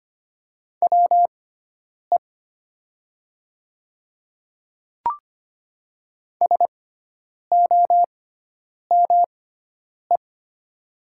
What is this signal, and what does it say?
Morse code, heard as a keyed tone, about 25 wpm, spaced as for 9 wpm, 700 Hz. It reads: WE SOME
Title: 6x Character Spacing - Top 100 Words - 25wpm